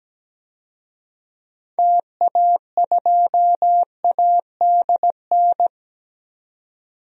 Morse code, heard as a keyed tone, about 17 wpm, 700 Hz. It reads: TA2ADN
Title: DXCC-100 Call Signs - 6 Characters - 17wpm